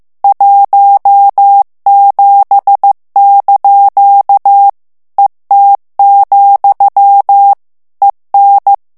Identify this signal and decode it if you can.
Morse code, heard as a keyed tone, about 15 wpm, 800 Hz. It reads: 17) ET, EN
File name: morse mp3 / FRANCAIS TEXTE CLAIR 1 VITESSE 14 MOTS MINUTES